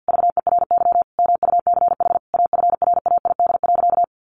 Morse code, heard as a keyed tone, about 50 wpm, 700 Hz. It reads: VERY MUCH AFRAID.
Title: Sentences from Top 1500 Words 50wpm